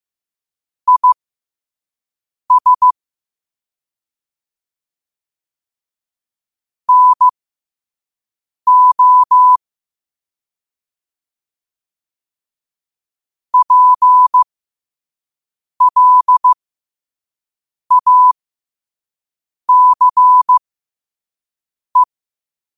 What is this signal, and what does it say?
Morse code, heard as a keyed tone, about 15 wpm, spaced as for 5 wpm, 1000 Hz. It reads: IS NO PLACE
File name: qotdfiles / 5wpm